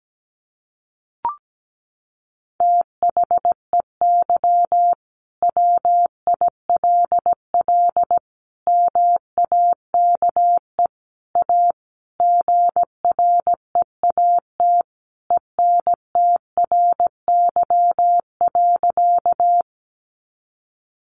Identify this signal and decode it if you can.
Morse code, heard as a keyed tone, about 17 wpm, 700 Hz. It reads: THEY WILL MAKE A GREAT ENTRY.
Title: Sentences from Top 1500 Words 17wpm